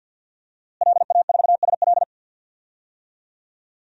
Morse code, heard as a keyed tone, about 52 wpm, 700 Hz. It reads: LA4SL
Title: DXCC-100 Call Signs - 5 Characters - Repeated 3x - 50wpm